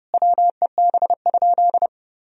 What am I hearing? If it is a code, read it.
Morse code, heard as a keyed tone, about 30 wpm, 700 Hz. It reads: WEB?
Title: Sentences from Top 1000 Words - Review Part 4 - 30wpm